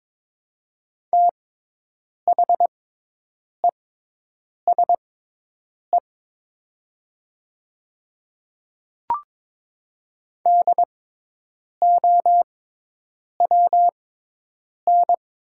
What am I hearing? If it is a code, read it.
Morse code, heard as a keyed tone, about 22 wpm, spaced as for 8 wpm, 700 Hz. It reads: THESE DOWN
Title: 6x Character Spacing - Top 200 Words - 22wpm